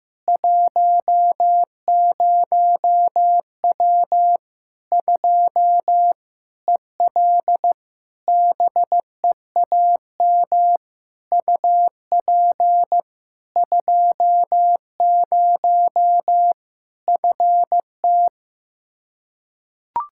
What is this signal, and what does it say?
Morse code, heard as a keyed tone, about 15 wpm, 700 Hz. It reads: 10W 2 EL BEAM UP 20 FT
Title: Practice QSO - Exchange 2 - Info - 15wpm